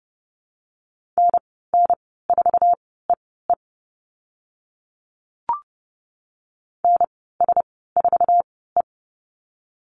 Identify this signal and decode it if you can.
Morse code, heard as a keyed tone, about 30 wpm, spaced as for 17 wpm, 700 Hz. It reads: NN4EE NS4E
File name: Lesson 15g - Number 5 - Call Signs 30wpm